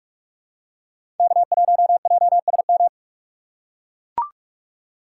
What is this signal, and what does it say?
Morse code, heard as a keyed tone, about 44 wpm, 690 Hz. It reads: K1JSM
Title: USA Call Signs - 5 Characters - 6x Character Spacing - 45wpm